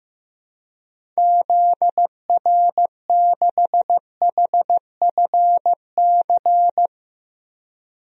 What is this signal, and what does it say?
Morse code, heard as a keyed tone, about 15 wpm, 700 Hz. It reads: ZR6HFC
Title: DXCC-100 Call Signs - 6 Characters - 15wpm